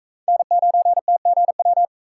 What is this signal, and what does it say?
Morse code, heard as a keyed tone, about 43 wpm, 700 Hz. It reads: N9TGW